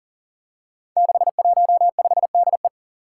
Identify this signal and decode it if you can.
Morse code, heard as a keyed tone, about 40 wpm, 700 Hz. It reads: 615DE
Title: Five Challenging Letter-Number Combinations 40wpm